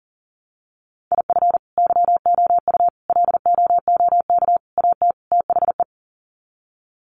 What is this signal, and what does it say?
Morse code, heard as a keyed tone, about 40 wpm, 700 Hz. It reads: IF YOU LOOK AT THE
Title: Sets of 5 Words 40wpm